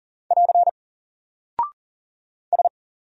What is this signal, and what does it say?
Morse code, heard as a keyed tone, about 40 wpm, 700 Hz. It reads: <AR> S
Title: Single Character (Letter:Number:Punctuation:Prosign) Alt 40wpm